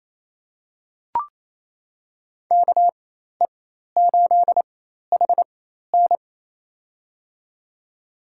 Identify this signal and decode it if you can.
Morse code, heard as a keyed tone, about 28 wpm, spaced as for 13 wpm, 700 Hz. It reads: KE8HN
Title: USA Call Signs - 5 Characters - 4x Character Spacing - 28wpm